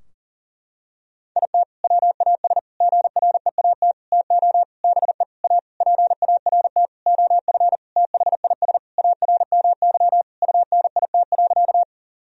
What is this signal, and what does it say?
Morse code, heard as a keyed tone, about 40 wpm, 700 Hz. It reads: IT WAS GREAT TO BE A PART OF THIS ARMY UNIT.
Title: Sentences from Top 1000 Words - Review Part 2 - 40wpm